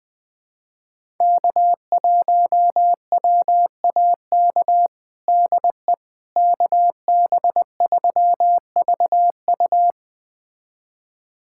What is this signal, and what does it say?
Morse code, heard as a keyed tone, about 21 wpm, 700 Hz. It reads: K1WAK DE KB3VU
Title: USA Callsign DE Callsign - 5 Characters - 20wpm